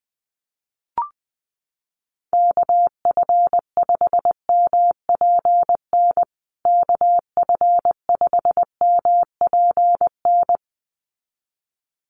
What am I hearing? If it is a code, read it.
Morse code, heard as a keyed tone, about 20 wpm, 700 Hz. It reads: KF5MPN KF5MPN
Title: USA Call Signs Repeated 2x 20wpm